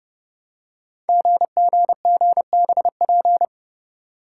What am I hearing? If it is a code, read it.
Morse code, heard as a keyed tone, about 28 wpm, 700 Hz. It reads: GGGBP
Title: Code Groups - 5 Letters - 30wpm